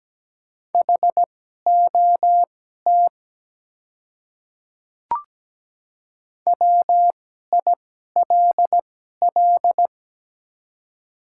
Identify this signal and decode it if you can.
Morse code, heard as a keyed tone, about 17 wpm, spaced as for 12 wpm, 690 Hz. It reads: HOT WILL
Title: 2x Character Spacing - Top 100 Words - 17wpm